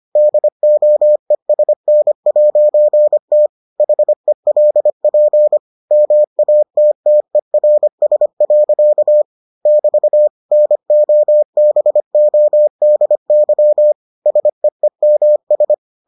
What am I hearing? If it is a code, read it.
Morse code, heard as a keyed tone, about 25 wpm, 600 Hz. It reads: DOESN'T HELP MATTERS. <BT> NOBODY SEEMS